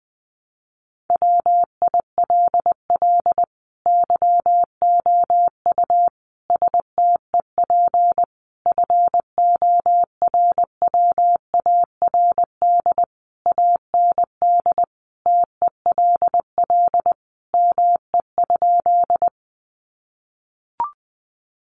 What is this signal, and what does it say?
Morse code, heard as a keyed tone, about 20 wpm, 700 Hz. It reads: WILL YOU STEP FORWARD AND TELL ME?
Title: Sentences from Top 1300 Words 20wpm